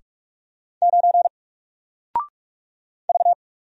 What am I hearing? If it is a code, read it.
Morse code, heard as a keyed tone, about 44 wpm, 700 Hz. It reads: 9 V
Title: Single Letter-Number 45wpm